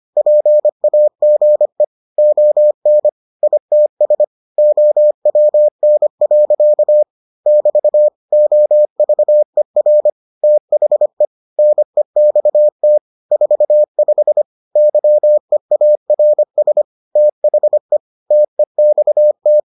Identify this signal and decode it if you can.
Morse code, heard as a keyed tone, about 25 wpm, 600 Hz. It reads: PAGE ON ITS OWN. <BT> OVER THE NEXT 45 YEARS THE TEXT